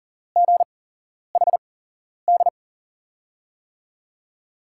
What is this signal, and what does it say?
Morse code, heard as a keyed tone, about 39 wpm, spaced as for 11 wpm, 700 Hz. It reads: GHD
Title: USA Call Signs - 5 Characters - 8x Character Spacing - 40wpm